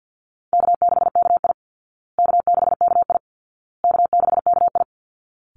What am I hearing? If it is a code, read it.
Morse code, heard as a keyed tone, about 50 wpm, 700 Hz. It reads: K6KI K6KI K6KI
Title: Easy Call Signs Repeated 3x - 50wpm